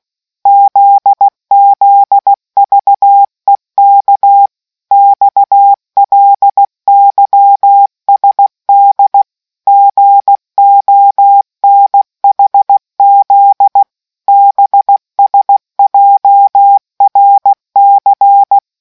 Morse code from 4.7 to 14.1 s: XLYSD GONHZ

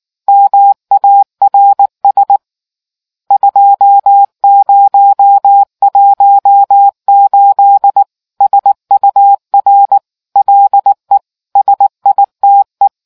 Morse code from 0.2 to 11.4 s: MARS 2018 SUR LE

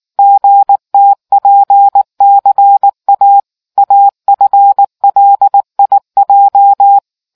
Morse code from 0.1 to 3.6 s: GTPCA

